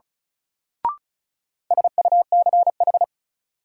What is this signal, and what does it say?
Morse code, heard as a keyed tone, about 36 wpm, 700 Hz. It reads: SUCH
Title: Sentences from Top 200 Words - New Word Drill - 35wpm